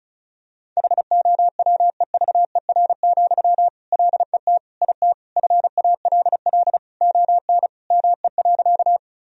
Morse code, heard as a keyed tone, about 35 wpm, 700 Hz. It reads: HOWEVER, LET IT FALL ON ME.
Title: Sentences from Top 1000 Words - Review Part 2 - 35wpm